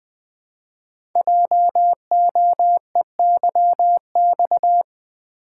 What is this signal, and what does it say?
Morse code, heard as a keyed tone, about 20 wpm, 700 Hz. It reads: JOEYX
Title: Code Groups - 5 Letter:Numbers - 20wpm